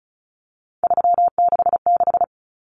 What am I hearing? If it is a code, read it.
Morse code, heard as a keyed tone, about 34 wpm, 700 Hz. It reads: 366
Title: Numbers - 3-Digit 35wpm